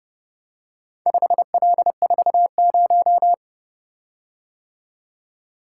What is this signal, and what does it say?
Morse code, heard as a keyed tone, about 29 wpm, 700 Hz. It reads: 5L40